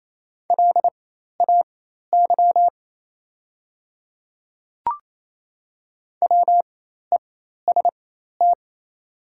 Morse code, heard as a keyed tone, about 28 wpm, spaced as for 13 wpm, 700 Hz. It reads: LAY WEST
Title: 4x Character Spacing - Top 500 Words - 28wpm